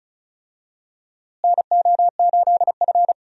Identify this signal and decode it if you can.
Morse code, heard as a keyed tone, about 36 wpm, 700 Hz. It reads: NO8F
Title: USA Call Signs - 4 Characters - Repeated 3x - 35wpm